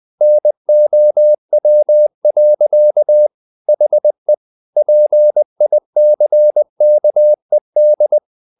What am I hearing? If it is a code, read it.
Morse code, heard as a keyed tone, about 20 wpm, 600 Hz. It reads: NOW. HE PICKED
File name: the old man and the sea / the old man and the sea part 1 20wpm